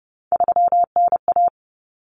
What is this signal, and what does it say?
Morse code, heard as a keyed tone, about 30 wpm, 700 Hz. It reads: 3NA
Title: Common Cut Numbers - 3-Digit 30wpm